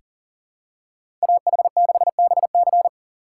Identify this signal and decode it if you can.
Morse code, heard as a keyed tone, about 40 wpm, 700 Hz. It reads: AH6BC
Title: USA Call Signs - 5 Characters - 4x Character Spacing - 40wpm